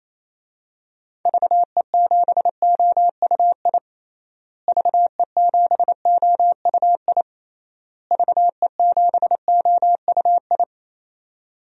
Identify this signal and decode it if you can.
Morse code, heard as a keyed tone, about 28 wpm, 700 Hz. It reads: VE7OUI VE7OUI VE7OUI
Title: DXCC-100 Call Signs - 6 Characters - Repeated 3x - 28wpm